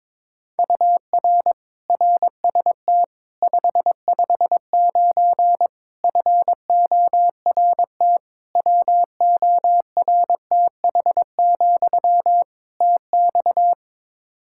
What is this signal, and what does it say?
Morse code, heard as a keyed tone, about 22 wpm, 700 Hz. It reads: UR RST 559 FORT WORTH, TX
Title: Practice QSO - Exchange 1 - Beginning - 22wpm